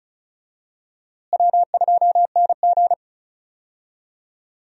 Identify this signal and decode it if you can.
Morse code, heard as a keyed tone, about 34 wpm, 700 Hz. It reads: W2NG
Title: USA Call Signs - 4 Characters - Repeated 3x - 35wpm